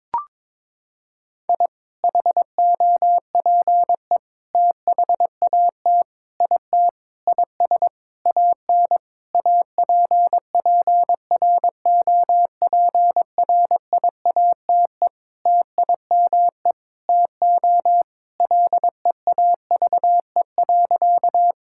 Morse code, heard as a keyed tone, about 22 wpm, 700 Hz. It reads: I HOPE THAT IT IS AN APPROPRIATE TIME TO LEAVE.
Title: Sentences from Top 1000 Words - Review Part 5 - 22wpm